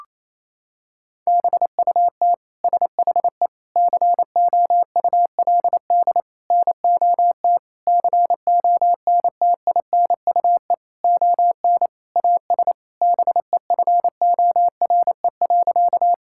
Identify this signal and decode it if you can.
Morse code, heard as a keyed tone, about 27 wpm, 700 Hz. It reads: BUT SHE COULD NOT CONTINUE ON AS BEFORE.
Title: Sentences from Top 1000 Words - Review Part 6 - 28wpm